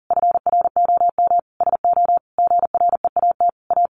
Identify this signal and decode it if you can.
Morse code, heard as a keyed tone, about 40 wpm, 700 Hz. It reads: FROM SO GREAT A
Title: Sentences from Top 1200 Words 40wpm